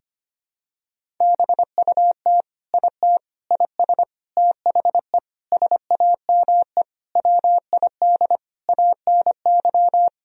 Morse code, heard as a keyed tone, about 25 wpm, 700 Hz. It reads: BUT IT IS THE SAME WID ANY
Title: Sentences from Top 300 Words and Abbreviations 25wpm